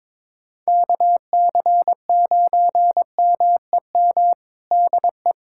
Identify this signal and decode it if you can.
Morse code, heard as a keyed tone, about 22 wpm, 700 Hz. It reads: KC9MEM DE
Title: USA Callsign DE Callsign - 6 Characters - 22wpm